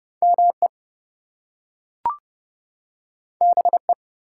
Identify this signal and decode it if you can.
Morse code, heard as a keyed tone, about 30 wpm, 700 Hz. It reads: ME BE
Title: Two Letter Words - 30wpm